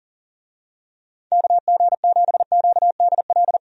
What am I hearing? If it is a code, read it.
Morse code, heard as a keyed tone, about 40 wpm, 700 Hz. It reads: KG7QDL
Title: USA Call Signs - 6 Characters - 40wpm